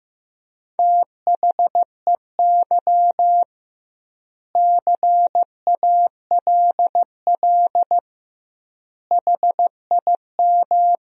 Morse code, heard as a keyed tone, about 16 wpm, 700 Hz. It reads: THEY CALL HIM
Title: 2x Word-Spacing - Sentences from Top 500 Words - 15wpm